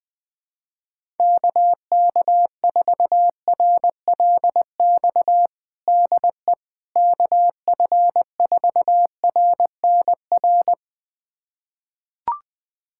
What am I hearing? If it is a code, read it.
Morse code, heard as a keyed tone, about 21 wpm, 700 Hz. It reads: KK4RLX DE KF4RNR
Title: USA Callsign DE Callsign - 6 Characters - 20wpm